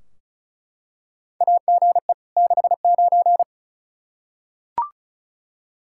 Morse code, heard as a keyed tone, about 34 wpm, 700 Hz. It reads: AGE 69
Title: Practice QSO - Exchange 3 - More Info - 35wpm